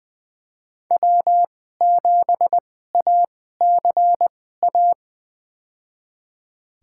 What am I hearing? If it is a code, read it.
Morse code, heard as a keyed tone, about 20 wpm, spaced as for 15 wpm, 700 Hz. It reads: W7ACA
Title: USA Call Signs - 5 Characters - 2x Character Spacing - 20wpm